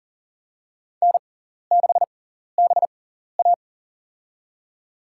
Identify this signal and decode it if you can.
Morse code, heard as a keyed tone, about 38 wpm, spaced as for 14 wpm, 700 Hz. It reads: N6BA
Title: USA Call Signs - 4 Characters - 6x Character Spacing - 40wpm